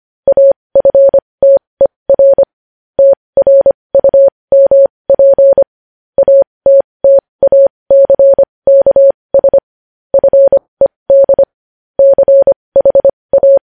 Morse code, heard as a keyed tone, about 25 wpm, spaced as for 20 wpm, 550 Hz. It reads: AFTER TRUMP ATTACKS FED CHA
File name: podcast / News headlines-1x-25x20